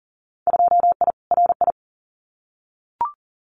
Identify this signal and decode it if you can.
Morse code, heard as a keyed tone, about 40 wpm, 700 Hz. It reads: 2I RI
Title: Winter Field Day Exchanges Encore 40wpm